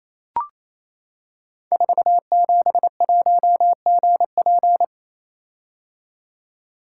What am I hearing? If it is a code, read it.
Morse code, heard as a keyed tone, about 27 wpm, 700 Hz. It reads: 471GP
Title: Code Groups - 5 Letter:Numbers - 28wpm